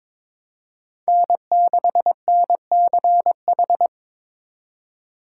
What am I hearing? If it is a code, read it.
Morse code, heard as a keyed tone, about 22 wpm, 700 Hz. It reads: N6NCH
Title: USA Call Signs - 4x Character Spacing - 22wpm